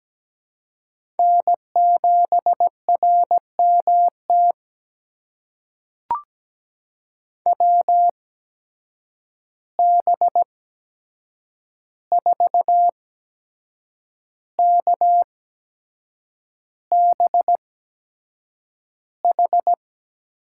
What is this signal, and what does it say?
Morse code, heard as a keyed tone, about 17 wpm, spaced as for 7 wpm, 700 Hz. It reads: N7RMT WB4KBH
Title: USA Call Signs - 8x Character Spacing - 17wpm